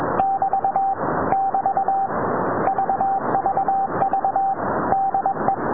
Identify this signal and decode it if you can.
Morse code, heard as a keyed tone, about 21 wpm, 780 Hz, with noise. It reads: <BT> <BT> VVV DE